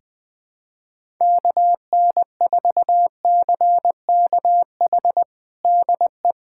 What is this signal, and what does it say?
Morse code, heard as a keyed tone, about 20 wpm, 700 Hz. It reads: KN4CKH DE
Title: USA Callsign DE Callsign 20wpm